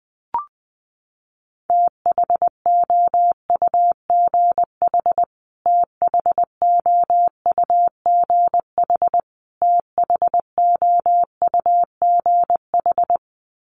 Morse code, heard as a keyed tone, about 20 wpm, 700 Hz. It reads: THOUGH THOUGH THOUGH
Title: Top 500 Words Repeated 20wpm